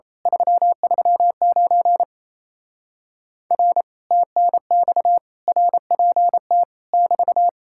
Morse code, heard as a keyed tone, about 27 wpm, 700 Hz. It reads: 339 R TNX RPT <BT>
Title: Speed-Racing Common QSO Phrases - 28wpm